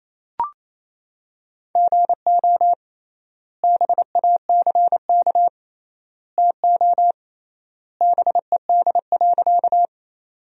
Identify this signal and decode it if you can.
Morse code, heard as a keyed tone, about 27 wpm, 700 Hz. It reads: GO BACK TO BED.
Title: 3x Word-Spacing - Sentences from Top 500 Words - 28wpm